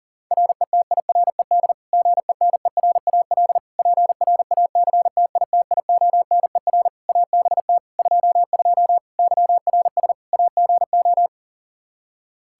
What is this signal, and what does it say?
Morse code, heard as a keyed tone, about 40 wpm, 700 Hz. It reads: RETIRED GENERAL PRACTITIONER ABT 22 YRS AGO
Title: Practice QSO - Exchange 3 - More Info - 40wpm